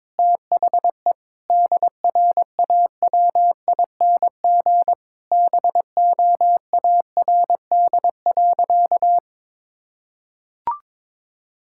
Morse code, heard as a keyed tone, about 22 wpm, 700 Hz. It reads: THE DRAWING BOARD.
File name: Sentences with 7-Word Phrases - 22wpm